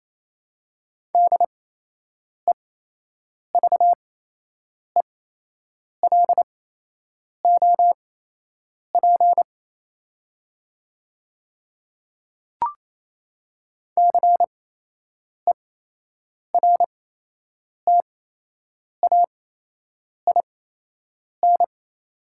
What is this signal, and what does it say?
Morse code, heard as a keyed tone, about 28 wpm, spaced as for 8 wpm, 700 Hz. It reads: DEVELOP CERTAIN
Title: 8x Character Spacing - Top 500 Words - 28wpm